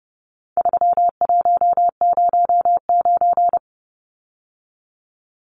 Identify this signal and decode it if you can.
Morse code, heard as a keyed tone, about 30 wpm, 700 Hz. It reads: 3109
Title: Numbers - 4-Digit 30wpm